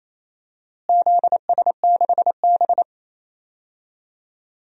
Morse code, heard as a keyed tone, about 28 wpm, 700 Hz. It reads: ZS6B
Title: DXCC-100 Call Signs - 4 Characters - Repeated 3x - 28wpm